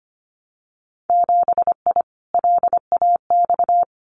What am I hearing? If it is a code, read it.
Morse code, heard as a keyed tone, about 25 wpm, 700 Hz. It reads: 7I LAX
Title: Winter Field Day Exchanges Encore 25wpm